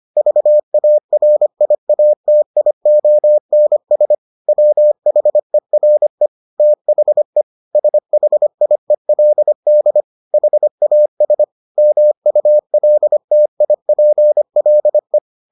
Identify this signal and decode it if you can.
Morse code, heard as a keyed tone, about 25 wpm, 600 Hz. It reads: VARIATIONS WHERE THE SHIELD HAS MULTIPLE